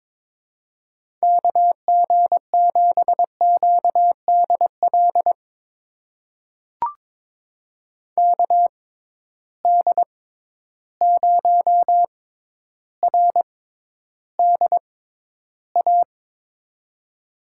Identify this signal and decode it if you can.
Morse code, heard as a keyed tone, about 21 wpm, spaced as for 11 wpm, 700 Hz. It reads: KG7QDL KD0RDA
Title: USA Call Signs - 6 Characters - 6x Character Spacing - 22wpm